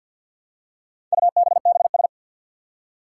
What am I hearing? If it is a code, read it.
Morse code, heard as a keyed tone, about 47 wpm, 700 Hz. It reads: UBBS
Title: Four Challenging Letter-Number Combinations 50wpm